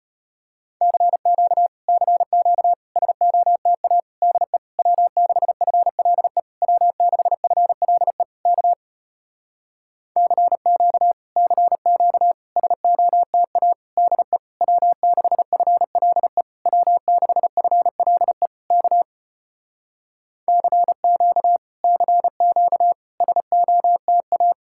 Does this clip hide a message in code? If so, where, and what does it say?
Yes: Morse code from 0.8 to 24.5 s, heard as a keyed tone, about 33 wpm, 700 Hz. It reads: CQ CQ SOTA DE W6FLE W6FLE K CQ CQ SOTA DE W6FLE W6FLE K CQ CQ SOTA